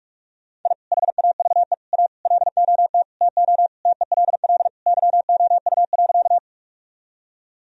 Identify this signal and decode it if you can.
Morse code, heard as a keyed tone, about 45 wpm, 700 Hz. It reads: I HAVE A LOT TO TELL YOU.
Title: Sentences from Top 1000 Words - Review Part 3 - 45wpm